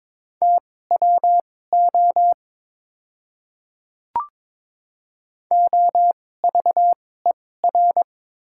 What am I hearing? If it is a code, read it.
Morse code, heard as a keyed tone, about 22 wpm, spaced as for 16 wpm, 700 Hz. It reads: TWO OVER
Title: 2x Character Spacing - Top 200 Words - 22wpm